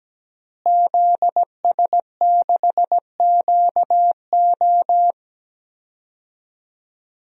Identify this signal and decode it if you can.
Morse code, heard as a keyed tone, about 17 wpm, 700 Hz. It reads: ZS6QO
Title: DXCC-100 Call Signs - 5 Characters - Repeated 3x - 17wpm